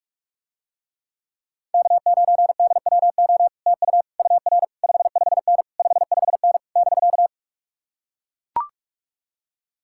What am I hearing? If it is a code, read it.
Morse code, heard as a keyed tone, about 44 wpm, 700 Hz. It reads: K9DWO TU UR 55N 55N <BK>